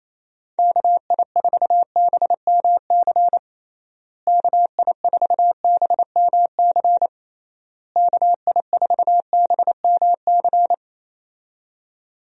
Morse code, heard as a keyed tone, about 28 wpm, 700 Hz. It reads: KI4BMC KI4BMC KI4BMC
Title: USA Call Signs - 6 Characters - Repeated 3x - 28wpm